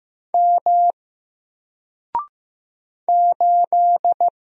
Morse code, heard as a keyed tone, about 15 wpm, 700 Hz. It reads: M 8
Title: Single Character (Letter:Number:Punctuation:Prosign) 15wpm